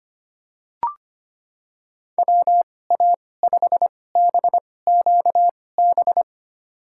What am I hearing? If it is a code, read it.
Morse code, heard as a keyed tone, about 25 wpm, spaced as for 18 wpm, 700 Hz. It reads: WA5BQB
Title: USA Call Signs - 6 Characters - 2x Character Spacing - 25wpm